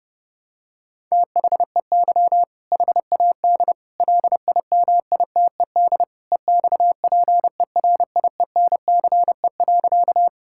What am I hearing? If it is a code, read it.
Morse code, heard as a keyed tone, about 30 wpm, 700 Hz. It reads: THEY HAD LIMITED EXPERIENCE.
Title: Sentences from Top 1000 Words - Review Part 6 - 30wpm